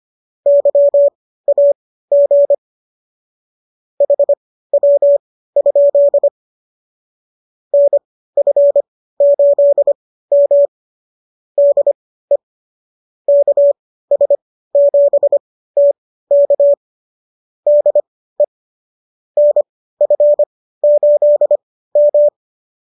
Morse code, heard as a keyed tone, about 25 wpm, spaced as for 15 wpm, 570 Hz. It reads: YAG HW? NF8M DE KS7T K DE NF8M